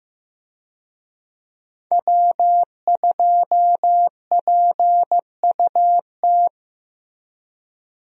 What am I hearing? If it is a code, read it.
Morse code, heard as a keyed tone, about 15 wpm, 700 Hz. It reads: W2PUT
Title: USA Call Signs - 5 Characters - Repeated 2x - 15wpm